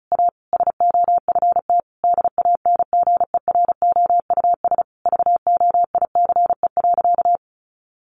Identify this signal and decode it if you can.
Morse code, heard as a keyed tone, about 34 wpm, 700 Hz. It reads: A SOFT DANGEROUS VOICE.
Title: Sentences from Top 1500 Words 35wpm